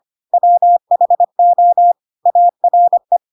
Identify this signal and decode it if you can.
Morse code, heard as a keyed tone, about 25 wpm, 700 Hz. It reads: WHO ARE
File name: Pinocchio / Pinocchio ch4 25 25 ik1hge